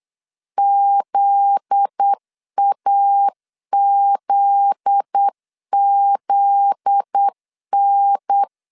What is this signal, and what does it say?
Morse code, heard as a keyed tone, about 8 wpm, 800 Hz. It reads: ZAZZN
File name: LEZIONI / 7 GF9Z / 7 GF9Z 8wpm 3213